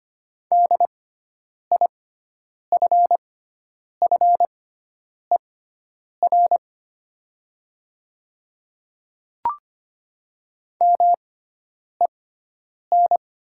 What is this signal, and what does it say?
Morse code, heard as a keyed tone, about 25 wpm, spaced as for 9 wpm, 700 Hz. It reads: DIFFER MEN
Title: 6x Character Spacing - Top 200 Words - 25wpm